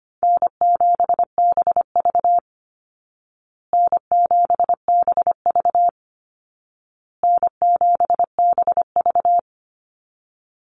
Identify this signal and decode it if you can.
Morse code, heard as a keyed tone, about 25 wpm, 700 Hz. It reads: N7BV N7BV N7BV
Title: Call Signs Repeated 2x - 25wpm